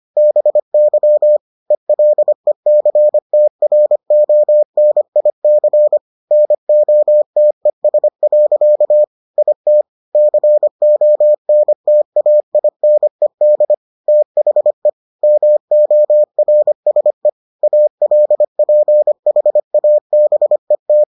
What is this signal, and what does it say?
Morse code, heard as a keyed tone, about 25 wpm, 600 Hz. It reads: BY ELECTRONIC NOTES. IT CONTAINED THE MORSE ALPHABET